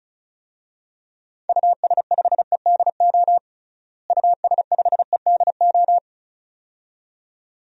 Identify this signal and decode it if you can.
Morse code, heard as a keyed tone, about 36 wpm, 700 Hz. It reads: US5EDO US5EDO
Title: DXCC-100 Call Signs - 6 Characters - Repeated 2x - 35wpm